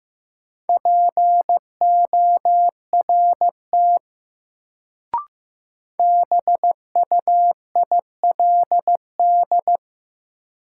Morse code, heard as a keyed tone, about 15 wpm, 700 Hz. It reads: PORT BUILD
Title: Sentences from Top 1000 Words - Review Part 1 - New Word Drill - 15wpm